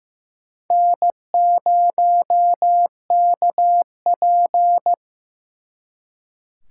Morse code, heard as a keyed tone, about 18 wpm, 700 Hz. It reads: N0KP
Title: Speed Racing - USA 1x2 Call Signs 15wpm